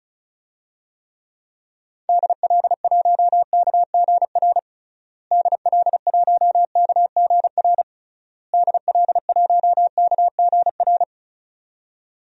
Morse code, heard as a keyed tone, about 34 wpm, 700 Hz. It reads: DL1KGR DL1KGR DL1KGR
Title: DXCC-100 Call Signs - 6 Characters - Repeated 3x - 35wpm